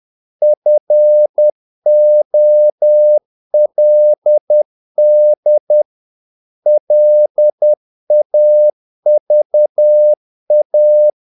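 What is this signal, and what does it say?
Morse code, heard as a keyed tone, about 10 wpm, 600 Hz. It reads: FOLD LAVA
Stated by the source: Morse practice of short Danish words at 10 wpm, from CW med Gnister.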